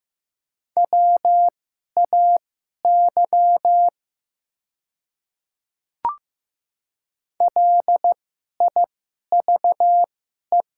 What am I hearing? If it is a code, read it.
Morse code, heard as a keyed tone, about 15 wpm, spaced as for 11 wpm, 700 Hz. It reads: WAY LIVE
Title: 2x Character Spacing - Top 200 Words - 15wpm